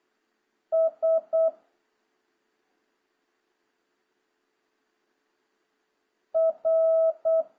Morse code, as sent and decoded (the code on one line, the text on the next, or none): ... / .-.
S R